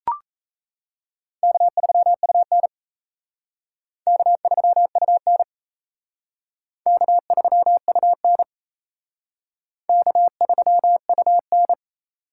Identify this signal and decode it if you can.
Morse code, heard as a keyed tone, about 35 wpm, 700 Hz. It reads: K3UN K3UN K3UN K3UN
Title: Speed Racing - USA 1x2 Call Signs 28wpm